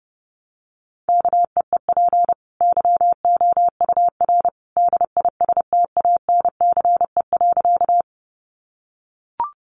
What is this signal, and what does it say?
Morse code, heard as a keyed tone, about 30 wpm, 700 Hz. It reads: KEEP YOUR DISTANCE.
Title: Sentences from Top 1200 Words 30wpm